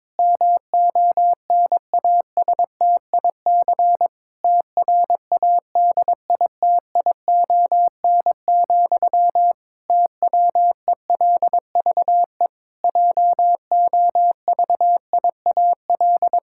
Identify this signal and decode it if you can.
Morse code, heard as a keyed tone, about 21 wpm, 700 Hz. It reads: MONASTIC TRADITION, TWELVE JOVIAL
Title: Pangram Challenge v2 - 22wpm